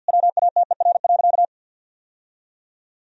Morse code, heard as a keyed tone, about 51 wpm, 700 Hz. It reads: WATER.
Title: Sentences from Top 1000 Words - Review Part 1 - 50wpm